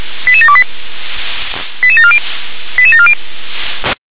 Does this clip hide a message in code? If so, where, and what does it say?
Yes: five-tone selective call (ZVEI) from 0.3 to 3.1 s, heard as quick runs of five tones, tones of about 70 ms. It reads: 88728 88740 88740